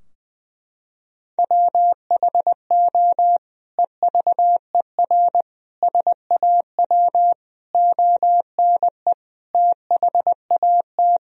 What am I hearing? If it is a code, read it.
Morse code, heard as a keyed tone, about 20 wpm, 700 Hz. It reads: WHO EVER SAW ONE THAT
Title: Sentences from Top 1000 Words - Review Part 2 - 20wpm